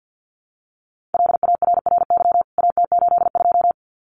Morse code, heard as a keyed tone, about 50 wpm, 700 Hz. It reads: LARRY AT82
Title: CWT Exchange 50wpm